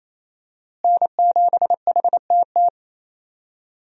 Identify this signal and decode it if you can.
Morse code, heard as a keyed tone, about 28 wpm, 700 Hz. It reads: N7HTT
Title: USA Call Signs - 5 Characters - Repeated 2x - 28wpm